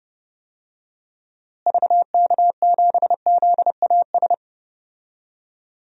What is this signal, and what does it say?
Morse code, heard as a keyed tone, about 30 wpm, 700 Hz. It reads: VK7ZAS
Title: DXCC-100 Call Signs - 6 Characters - Repeated 2x - 30wpm